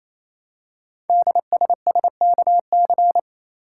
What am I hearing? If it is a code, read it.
Morse code, heard as a keyed tone, about 29 wpm, 700 Hz. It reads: DSSKC
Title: Code Groups - 5 Letters - 28wpm